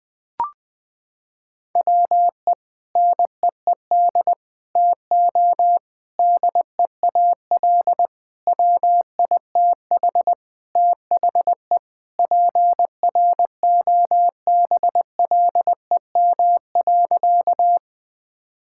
Morse code, heard as a keyed tone, about 21 wpm, 700 Hz. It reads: WE NEED TO DEAL WITH THE PROBLEM.
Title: Sentences with 4-Word Phrases - 20wpm